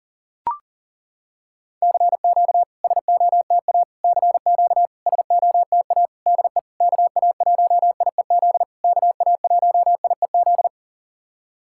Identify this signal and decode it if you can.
Morse code, heard as a keyed tone, about 40 wpm, 690 Hz. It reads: CQ SOTA CQ SOTA DE KA1IEZ KA1IEZ
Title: SOTA QSO Practice 40wpm